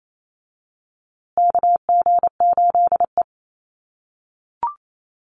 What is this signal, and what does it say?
Morse code, heard as a keyed tone, about 28 wpm, 700 Hz. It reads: KG8E
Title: USA Call Signs Repeated 3x 28wpm